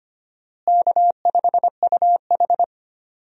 Morse code, heard as a keyed tone, about 25 wpm, 700 Hz. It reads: K5UH
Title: USA Call Signs - 4 Characters - Repeated 2x - 25wpm